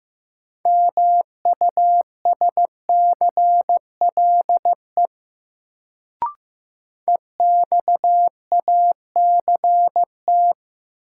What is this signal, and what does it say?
Morse code, heard as a keyed tone, about 15 wpm, 700 Hz. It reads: MUSCLE EXACT
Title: Sentences from Top 1300 Words - New Word Drill Alt - 15wpm